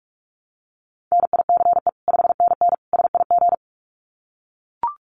Morse code, heard as a keyed tone, about 44 wpm, 700 Hz. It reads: NICE 5NN SIG
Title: Common QSO Phrases 45wpm